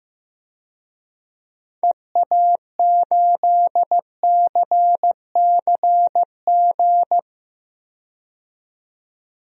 Morse code, heard as a keyed tone, about 15 wpm, 700 Hz. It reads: EA8CCG
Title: DXCC-100 Call Signs - 6 Characters - 15wpm